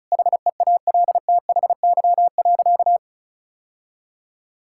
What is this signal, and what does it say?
Morse code, heard as a keyed tone, about 36 wpm, 700 Hz. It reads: HEALTHY.
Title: Sentences from Top 1000 Words - Review Part 5 - 35wpm